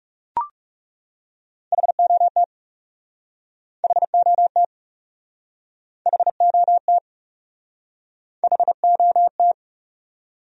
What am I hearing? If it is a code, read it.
Morse code, heard as a keyed tone, about 40 wpm, 700 Hz. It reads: HOT HOT HOT HOT